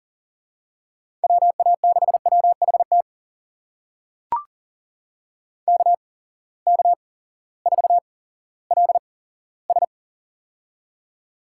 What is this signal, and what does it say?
Morse code, heard as a keyed tone, about 40 wpm, spaced as for 18 wpm, 700 Hz. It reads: WA6WHT KK4LS
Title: USA Call Signs - 8x Character Spacing - 40wpm